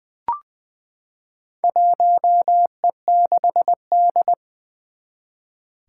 Morse code, heard as a keyed tone, about 21 wpm, 700 Hz. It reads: 1E6D
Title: Four Challenging Letter-Number Combinations 20wpm